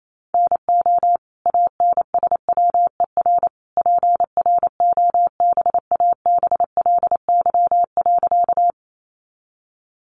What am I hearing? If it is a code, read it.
Morse code, heard as a keyed tone, about 28 wpm, 700 Hz. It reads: NO ANSWER PROBABLY.
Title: Sentences from Top 700 Words 28wpm